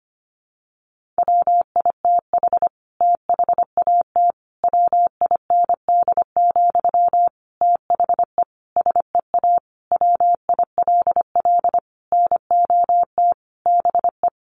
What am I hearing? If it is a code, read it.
Morse code, heard as a keyed tone, about 25 wpm, 700 Hz. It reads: WITH THAT WIND, THE SEA WILL NOT BE
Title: Sentences from Top 1400 Words 25wpm